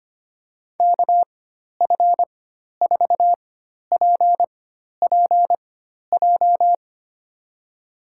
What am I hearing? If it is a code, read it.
Morse code, heard as a keyed tone, about 25 wpm, spaced as for 12 wpm, 700 Hz. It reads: KF4PPJ
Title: USA Call Signs - 4x Character Spacing - 25wpm